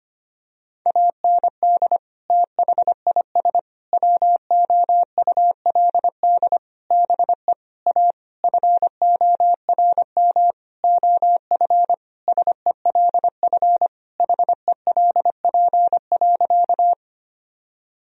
Morse code, heard as a keyed tone, about 25 wpm, 700 Hz. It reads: AND THIS WOULD BE A FORM OF SELF HELP.
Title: Sentences from Top 1000 Words - Review Part 1 - 25wpm